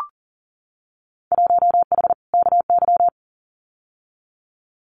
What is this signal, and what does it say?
Morse code, heard as a keyed tone, about 40 wpm, 700 Hz. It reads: 1H KY
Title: Winter Field Day Exchanges 40wpm